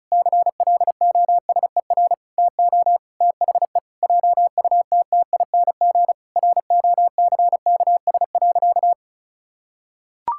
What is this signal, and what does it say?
Morse code, heard as a keyed tone, about 34 wpm, 700 Hz. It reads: CLOSER TO THE JUTTING ROCKS.
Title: Pangram Challenge v2 - 35wpm